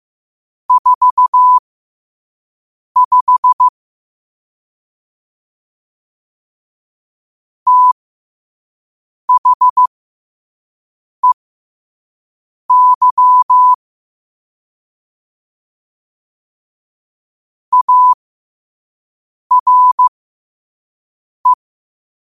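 Morse code, heard as a keyed tone, about 15 wpm, spaced as for 5 wpm, 1000 Hz. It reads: 45 THEY ARE
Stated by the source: Morse code sent at 5 wpm.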